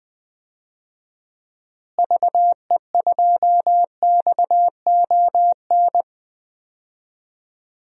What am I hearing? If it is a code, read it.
Morse code, heard as a keyed tone, about 20 wpm, 700 Hz. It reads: VE2XON
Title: DXCC-100 Call Signs - 6 Characters - Repeated 2x - 20wpm